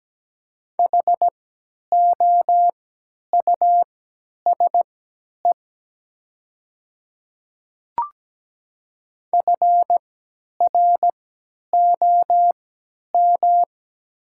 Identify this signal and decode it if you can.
Morse code, heard as a keyed tone, about 17 wpm, spaced as for 10 wpm, 700 Hz. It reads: HOUSE FROM
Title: 3x Character Spacing - Top 200 Words - 17wpm